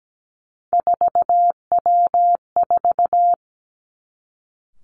Morse code, heard as a keyed tone, about 17 wpm, 700 Hz. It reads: 4W4
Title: Three Challenging Letter-Number Combinations 17wpm